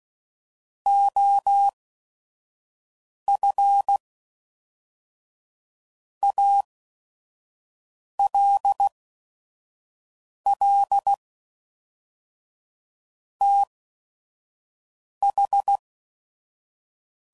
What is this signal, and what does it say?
Morse code, heard as a keyed tone, about 16 wpm, spaced as for 5 wpm, 800 Hz. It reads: OF ALL TH
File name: CW Code Groups @5WPM / Technical Subjects / AND Gate